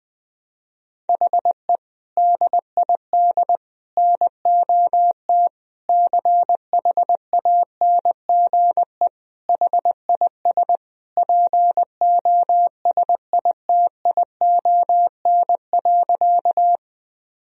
Morse code, heard as a keyed tone, about 20 wpm, 700 Hz. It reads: HE DID NOT CHANGE HIS POSITION.
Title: Sentences from Top 1000 Words - Review Part 1 - 20wpm